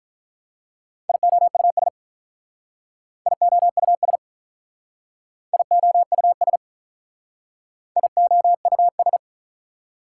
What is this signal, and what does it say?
Morse code, heard as a keyed tone, about 44 wpm, 700 Hz. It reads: IOUS IOUS IOUS IOUS